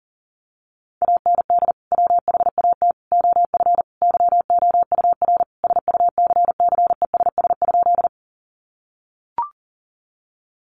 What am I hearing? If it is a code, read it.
Morse code, heard as a keyed tone, about 40 wpm, 700 Hz. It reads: AND WHAT OF YOUR SUCCESS?
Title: Sentences from Top 900 Words 40wpm